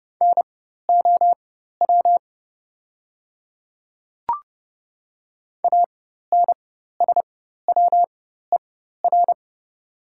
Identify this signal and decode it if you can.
Morse code, heard as a keyed tone, about 30 wpm, spaced as for 14 wpm, 700 Hz. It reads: NOW ANSWER
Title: 4x Character Spacing - Top 500 Words - 30wpm